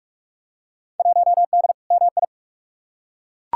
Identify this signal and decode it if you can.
Morse code, heard as a keyed tone, about 42 wpm, 700 Hz. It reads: 1D MI
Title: Field Day Exchanges - 45wpm